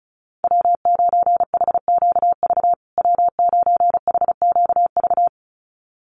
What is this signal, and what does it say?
Morse code, heard as a keyed tone, about 34 wpm, 700 Hz. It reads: W9HQV W9HQV
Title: USA Call Signs Repeated 2x 35wpm